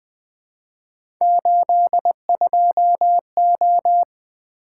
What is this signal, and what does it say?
Morse code, heard as a keyed tone, about 20 wpm, 700 Hz. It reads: 82O
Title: Three Letter-Number Combinations - 20wpm